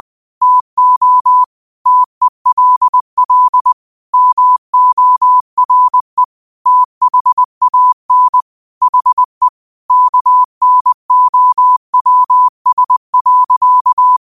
Morse code, heard as a keyed tone, about 20 wpm, 1000 Hz. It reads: TO TELL MORE THAN HE KNOWS.
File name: qotdfiles / 20wpm